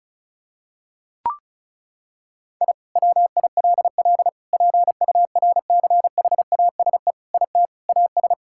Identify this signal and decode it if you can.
Morse code, heard as a keyed tone, about 36 wpm, 700 Hz. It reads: I WILL PURCHASE IT AS
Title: Sentences from Top 1000 Words - Review Part 5 - 35wpm